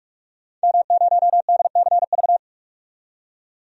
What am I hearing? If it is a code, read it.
Morse code, heard as a keyed tone, about 44 wpm, 700 Hz. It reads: M0DCV
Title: DXCC-100 Call Signs - 5 Characters - 45wpm